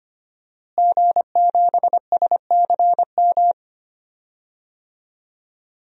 Morse code, heard as a keyed tone, about 25 wpm, 700 Hz. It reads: G7SCM
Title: DXCC-100 Call Signs - 5 Characters - Repeated 2x - 25wpm